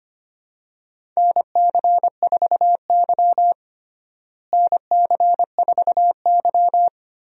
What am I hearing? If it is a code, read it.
Morse code, heard as a keyed tone, about 25 wpm, 700 Hz. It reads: NC4Y NC4Y
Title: USA Call Signs - 4 Characters - Repeated 2x - 25wpm